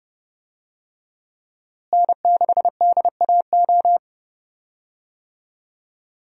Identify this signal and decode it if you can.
Morse code, heard as a keyed tone, about 32 wpm, 700 Hz. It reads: N6DAO
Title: USA Call Signs - 5 Characters - Repeated 2x - 30wpm